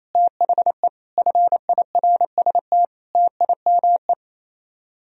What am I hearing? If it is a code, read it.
Morse code, heard as a keyed tone, about 32 wpm, 700 Hz. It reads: THE FIRST TIME
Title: Speed-Racing - Sets of 5 Words 28wpm